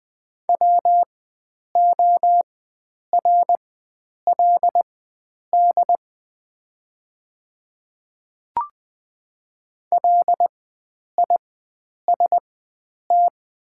Morse code, heard as a keyed tone, about 20 wpm, spaced as for 9 wpm, 700 Hz. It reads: WORLD LIST
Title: 4x Character Spacing - Top 500 Words - 20wpm